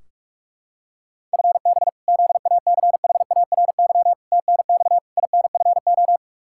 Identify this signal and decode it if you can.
Morse code, heard as a keyed tone, about 46 wpm, 700 Hz. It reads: FB ZACHARY TNX INFO
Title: Practice QSO - Exchange 3 - More Info - 45wpm